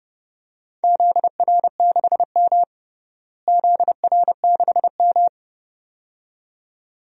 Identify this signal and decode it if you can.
Morse code, heard as a keyed tone, about 31 wpm, 700 Hz. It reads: ZR6M ZR6M
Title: DXCC-100 Call Signs - 4 Characters - Repeated 2x - 30wpm